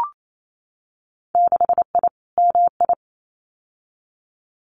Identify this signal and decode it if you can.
Morse code, heard as a keyed tone, about 28 wpm, 700 Hz. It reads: 6I MI
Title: Winter Field Day Exchanges Encore 28wpm